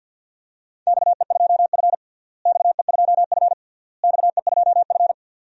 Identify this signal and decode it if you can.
Morse code, heard as a keyed tone, about 52 wpm, 700 Hz. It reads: XE2F XE2F XE2F
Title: DXCC-100 Call Signs - 4 Characters - Repeated 3x - 50wpm